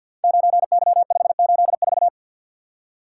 Morse code, heard as a keyed tone, about 50 wpm, 700 Hz. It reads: 9Y584
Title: Code Groups - 5 Letter:Numbers - 50wpm